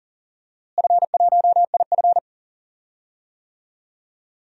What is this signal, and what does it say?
Morse code, heard as a keyed tone, about 40 wpm, 700 Hz. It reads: F1IF
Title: DXCC-100 Call Signs - 4 Characters - Repeated 3x - 40wpm